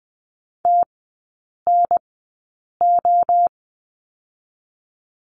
Morse code, heard as a keyed tone, about 20 wpm, 700 Hz. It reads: T N O
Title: 029 Lesson 6e - Letter i - Three Characters 20wpm